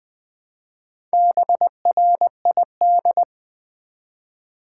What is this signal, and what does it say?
Morse code, heard as a keyed tone, about 20 wpm, 700 Hz. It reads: BRID
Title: Build A Word - 1 Letter At A Time - 5-Letter Words - 20wpm